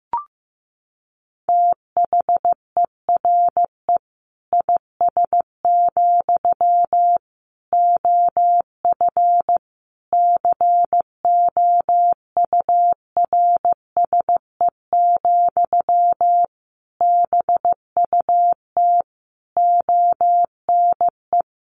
Morse code, heard as a keyed tone, about 15 wpm, 700 Hz. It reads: THERE IS, OF COURSE, BUT ONE
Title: Sentences from Top 1200 Words 15wpm